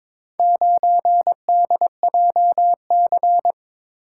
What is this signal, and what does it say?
Morse code, heard as a keyed tone, about 22 wpm, 700 Hz. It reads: 9DJC
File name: practice / 22wpm / Four Letter-Number Combinations - 22wpm